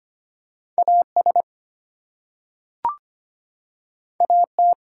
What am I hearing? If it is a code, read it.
Morse code, heard as a keyed tone, about 25 wpm, 700 Hz. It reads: AS AT
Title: Two Letter Words - 25wpm